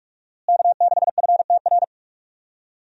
Morse code, heard as a keyed tone, about 46 wpm, 700 Hz. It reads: K6FTR
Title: USA Call Signs - 5 Characters - Repeated 3x - 45wpm